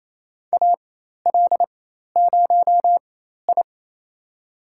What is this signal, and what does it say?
Morse code, heard as a keyed tone, about 28 wpm, spaced as for 13 wpm, 700 Hz. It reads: AL0I
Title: USA Call Signs - 4 Characters - 4x Character Spacing - 28wpm